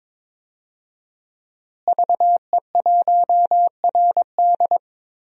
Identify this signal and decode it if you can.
Morse code, heard as a keyed tone, about 22 wpm, 700 Hz. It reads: VE1RD
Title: DXCC-100 Call Signs - 5 Characters - Repeated 3x - 22wpm